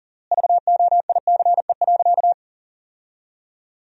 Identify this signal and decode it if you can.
Morse code, heard as a keyed tone, about 37 wpm, 700 Hz. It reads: VOICE.
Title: 3x Word-Spacing - Sentences from Top 500 Words - 40wpm